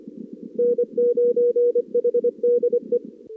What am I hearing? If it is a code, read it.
Morse code, heard as a keyed tone, about 24 wpm, 480 Hz. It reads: N9HDE
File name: CWSprint / Feb2022 / k5tr-feb-na-sprint-cw-k5ot -0100